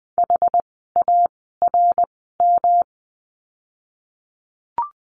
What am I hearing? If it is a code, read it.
Morse code, heard as a keyed tone, about 20 wpm, spaced as for 14 wpm, 700 Hz. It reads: HARM
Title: 108 Lesson 18f - Letter m - Words 20wpm